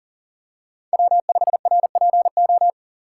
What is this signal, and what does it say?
Morse code, heard as a keyed tone, about 38 wpm, 700 Hz. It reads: W5RPO
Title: USA Call Signs - 5 Characters - 4x Character Spacing - 40wpm